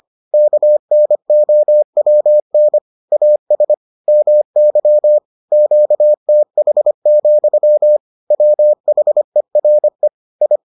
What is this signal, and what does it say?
Morse code, heard as a keyed tone, about 25 wpm, 600 Hz. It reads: KNOWN AS MY QTH, WHERE I